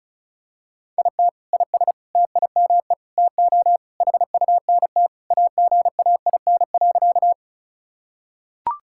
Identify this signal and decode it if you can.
Morse code, heard as a keyed tone, about 36 wpm, 700 Hz. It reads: IT IS TIME TO HUNT AGAIN.
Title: Sentences from Top 1600 Words 35wpm